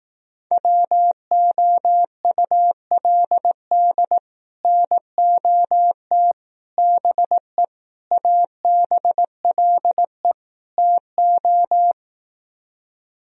Morse code, heard as a keyed tone, about 18 wpm, 700 Hz. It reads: WOULD NOT BE ABLE TO
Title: Speed-Racing - Sets of 5 Words 15wpm